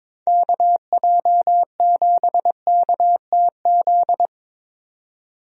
Morse code, heard as a keyed tone, about 22 wpm, 700 Hz. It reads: KJ7KTZ
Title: USA Call Signs - 6 Characters - 22wpm